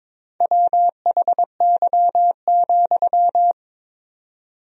Morse code, heard as a keyed tone, about 22 wpm, 700 Hz. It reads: WHY,
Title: 4x Word-Spacing - Sentences from Top 300 Words - 22wpm